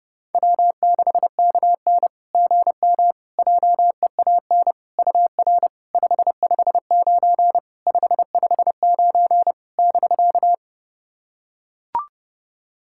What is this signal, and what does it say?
Morse code, heard as a keyed tone, about 30 wpm, 700 Hz. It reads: W6KN GM JEAN UR 559 559 <BK>
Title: SOTA QSO Practice 30wpm